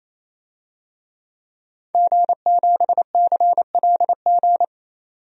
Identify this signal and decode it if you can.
Morse code, heard as a keyed tone, about 28 wpm, 700 Hz. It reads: G7CLG
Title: DXCC-100 Call Signs - 5 Characters - 28wpm